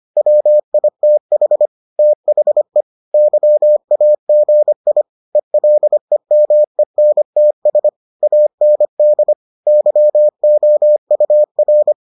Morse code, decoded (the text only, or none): WITH THE YAGI ELEMENTS AND YOUR